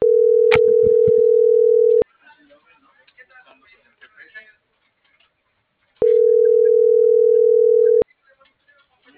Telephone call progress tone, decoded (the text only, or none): ringback tone